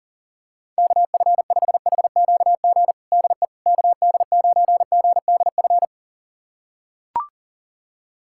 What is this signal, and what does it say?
Morse code, heard as a keyed tone, about 40 wpm, 700 Hz. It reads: KF5HQG DE KD9GDF
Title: USA Callsign DE Callsign 40wpm